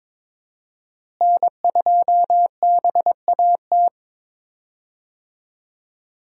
Morse code, heard as a keyed tone, about 22 wpm, 700 Hz. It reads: N2BAT
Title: USA Call Signs - 5 Characters - Repeated 2x - 22wpm